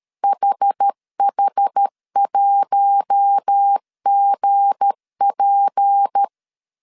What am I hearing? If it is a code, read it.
Morse code, heard as a keyed tone, about 13 wpm, 800 Hz. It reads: HH1GP
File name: CW / DET / 1 9 1500C S A 12wpm 3800